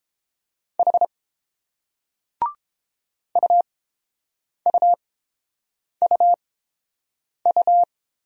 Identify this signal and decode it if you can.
Morse code, heard as a keyed tone, about 27 wpm, spaced as for 7 wpm, 700 Hz. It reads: H UUUU
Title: Speed-Racing Single Letters 22wpm